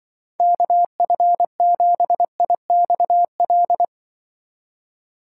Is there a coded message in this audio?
Yes: Morse code from 0.4 to 3.8 s, heard as a keyed tone, about 25 wpm, 700 Hz. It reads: KF7IXL